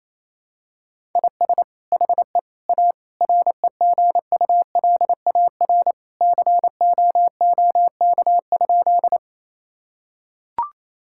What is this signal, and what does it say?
Morse code, heard as a keyed tone, about 28 wpm, 700 Hz. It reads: IS HE A REGULAR COOK?
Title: Sentences from Top 1000 Words - Review Part 6 - 28wpm